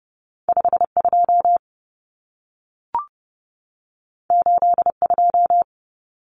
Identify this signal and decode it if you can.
Morse code, heard as a keyed tone, about 30 wpm, 700 Hz. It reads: 52 82
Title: Common Cut Numbers - 2-Digit 30wpm